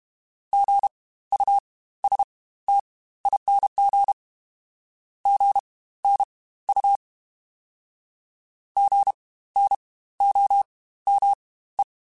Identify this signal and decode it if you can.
Morse code, heard as a keyed tone, about 32 wpm, spaced as for 16 wpm, 800 Hz. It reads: GUSTING GNU GNOME
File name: Spaced Alphas / Spaced Alphas mp3s / Gs sp @30